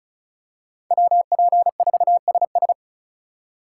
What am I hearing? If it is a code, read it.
Morse code, heard as a keyed tone, about 34 wpm, 700 Hz. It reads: WP4SS
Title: USA Call Signs - 3x Character Spacing - 35wpm